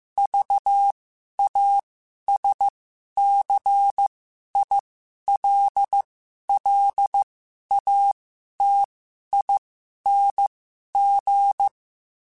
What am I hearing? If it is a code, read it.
Morse code, heard as a keyed tone, about 15 wpm, spaced as for 11 wpm, 800 Hz. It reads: VASCILLATING
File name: Special INGs-mp3s / Special INGs-mp3s / Sngl Ing @10